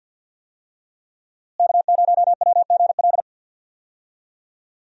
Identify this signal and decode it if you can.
Morse code, heard as a keyed tone, about 48 wpm, 700 Hz. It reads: K0WGL
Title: USA Call Signs - 5 Characters - Repeated 3x - 50wpm